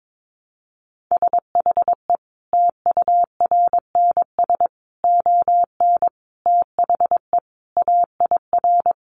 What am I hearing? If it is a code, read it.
Morse code, heard as a keyed tone, about 22 wpm, 700 Hz. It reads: SHE TURNS ON THE AIR
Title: Additional Easy Phrases 22wpm